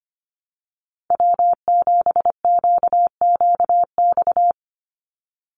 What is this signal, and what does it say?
Morse code, heard as a keyed tone, about 25 wpm, 700 Hz. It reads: W7QQX
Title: USA Call Signs 25wpm